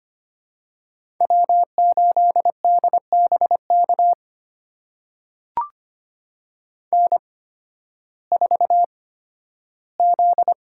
Morse code, heard as a keyed tone, about 25 wpm, spaced as for 13 wpm, 700 Hz. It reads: W8DBK N4Z
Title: USA Call Signs - 5 Characters - 8x Character Spacing - 25wpm